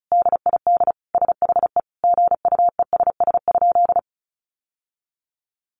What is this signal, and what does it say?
Morse code, heard as a keyed tone, about 34 wpm, 700 Hz. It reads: DID SHE GUESS?
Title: Sentences from Top 1500 Words 35wpm